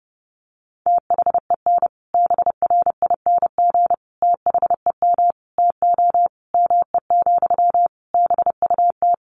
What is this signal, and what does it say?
Morse code, heard as a keyed tone, about 28 wpm, 700 Hz. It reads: THEN BRING THEM TO ME, BUT
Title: Sentences from Top 1500 Words 30wpm